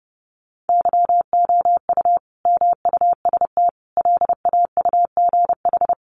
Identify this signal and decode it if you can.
Morse code, heard as a keyed tone, about 30 wpm, 700 Hz. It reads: YOU MUST LAUGH
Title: Sentences from Top 600 Words 30wpm